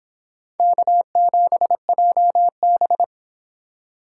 Morse code, heard as a keyed tone, about 26 wpm, 700 Hz. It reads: K7JB